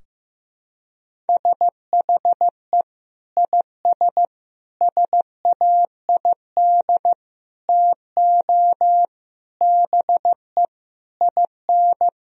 Morse code, heard as a keyed tone, about 16 wpm, 700 Hz. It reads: SHE IS SAID TO BE IN
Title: Sentences from Top 1000 Words - Review Part 2 - 15wpm